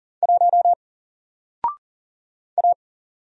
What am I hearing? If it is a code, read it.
Morse code, heard as a keyed tone, about 40 wpm, 700 Hz. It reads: 1 A
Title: Single Letter-Number Alt 40wpm